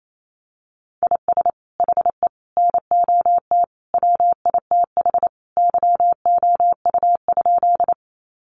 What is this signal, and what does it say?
Morse code, heard as a keyed tone, about 28 wpm, 700 Hz. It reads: IS HE NOT WITH YOU?
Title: Sentences from Top 600 Words 28wpm